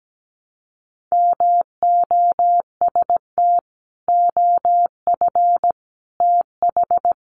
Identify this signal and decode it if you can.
Morse code, heard as a keyed tone, about 17 wpm, 700 Hz. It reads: MOST OF TH
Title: Sentences from Top 1200 Words 17wpm